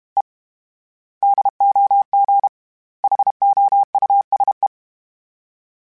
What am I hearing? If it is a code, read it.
Morse code, heard as a keyed tone, about 32 wpm, 800 Hz. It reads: E DOG HOUSE
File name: General / General-mp3s / Houses @30